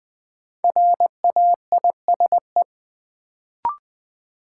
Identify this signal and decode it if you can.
Morse code, heard as a keyed tone, about 21 wpm, 700 Hz. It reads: RAISE
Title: Sentences from Top 1400 Words - New Word Drill Alt - 20wpm